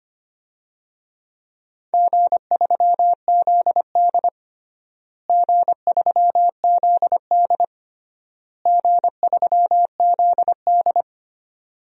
Morse code, heard as a keyed tone, about 25 wpm, 700 Hz. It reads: G3ZD G3ZD G3ZD
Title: DXCC-100 Call Signs - 4 Characters - Repeated 3x - 25wpm